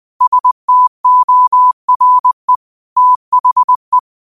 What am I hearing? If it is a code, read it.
Morse code, heard as a keyed tone, about 20 wpm, 1000 Hz. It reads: STORE THE
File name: qotdfiles / 20wpm